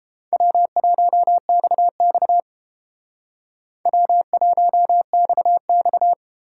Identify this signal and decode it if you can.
Morse code, heard as a keyed tone, about 29 wpm, 700 Hz. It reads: W1XX W1XX